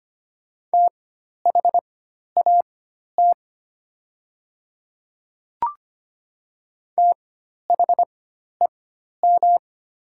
Morse code, heard as a keyed tone, about 25 wpm, spaced as for 12 wpm, 700 Hz. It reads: THAT THEM
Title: 4x Character Spacing - Top 100 Words - 25wpm